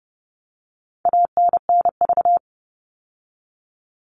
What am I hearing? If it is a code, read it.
Morse code, heard as a keyed tone, about 28 wpm, 700 Hz. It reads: ANNV
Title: All Cut Numbers - 4-Digit 30wpm